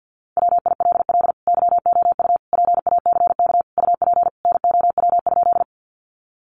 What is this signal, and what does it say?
Morse code, heard as a keyed tone, about 50 wpm, 700 Hz. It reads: WILL YOU PACK UP NOW?
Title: Sentences from Top 1300 Words 50wpm